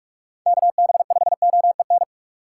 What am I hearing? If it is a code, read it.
Morse code, heard as a keyed tone, about 46 wpm, 700 Hz. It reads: KB5OEN
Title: USA Call Signs - 6 Characters - 4x Character Spacing - 45wpm